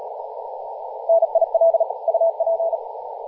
Morse code, heard as a keyed tone, about 35 wpm, 660 Hz, with radio noise. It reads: NIL UP
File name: posts / 2016-aug-26-2236z-yj0cow-18-073mhz-cw